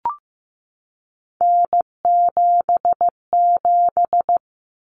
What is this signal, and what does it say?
Morse code, heard as a keyed tone, about 15 wpm, 700 Hz. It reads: N77
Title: Common Cut Numbers - 3-Digit 15wpm